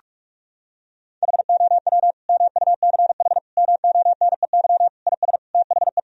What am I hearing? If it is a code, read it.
Morse code, heard as a keyed tone, about 46 wpm, 700 Hz. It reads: HOW MUCH MONEY IS THE
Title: Sentences from Top 1000 Words - Review Part 4 - 45wpm